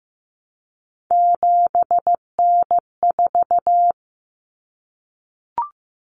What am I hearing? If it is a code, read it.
Morse code, heard as a keyed tone, about 15 wpm, 700 Hz. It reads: 7N4
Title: Common Cut Numbers - 3-Digit 15wpm